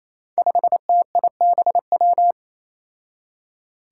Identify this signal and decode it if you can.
Morse code, heard as a keyed tone, about 28 wpm, 700 Hz. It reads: 5TIBW
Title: Five Challenging Letter-Number Combinations 28wpm